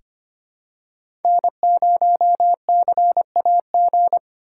Morse code, heard as a keyed tone, about 25 wpm, 700 Hz. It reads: N0CAG
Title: USA Call Signs - 5 Characters - 2x Character Spacing - 25wpm